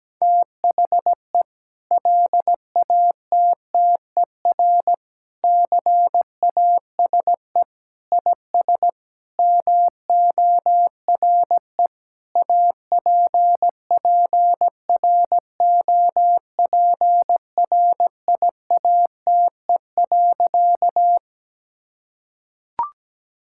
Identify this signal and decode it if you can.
Morse code, heard as a keyed tone, about 17 wpm, 700 Hz. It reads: THE LATTER CASE IS MORE APPROPRIATE.
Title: Sentences from Top 1600 Words 17wpm